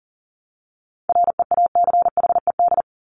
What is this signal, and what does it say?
Morse code, heard as a keyed tone, about 40 wpm, 700 Hz. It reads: REACHED
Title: ED Suffix - Top 100 Words with Base Words 40wpm